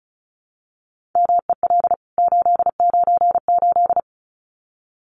Morse code, heard as a keyed tone, about 34 wpm, 700 Hz. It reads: MEL 898
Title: CWT Exchange 35wpm